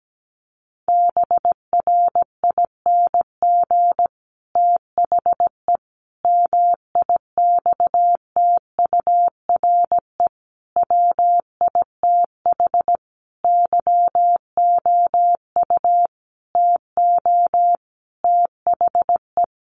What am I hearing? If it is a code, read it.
Morse code, heard as a keyed tone, about 17 wpm, 700 Hz. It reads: BRING THE MIXTURE WITH YOU TO THE
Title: Sentences from Top 1500 Words 17wpm